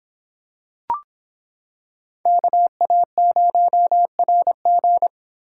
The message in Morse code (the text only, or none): KA0RG